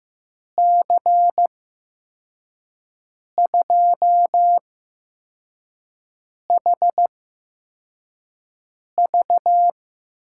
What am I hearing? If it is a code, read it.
Morse code, heard as a keyed tone, about 15 wpm, spaced as for 4 wpm, 700 Hz. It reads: C2HV
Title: USA Call Signs - 6 Characters - 8x Character Spacing - 15wpm